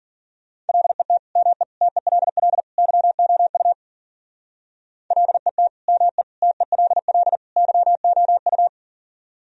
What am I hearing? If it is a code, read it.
Morse code, heard as a keyed tone, about 51 wpm, 700 Hz. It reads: LET ME TELL YOU LET ME TELL YOU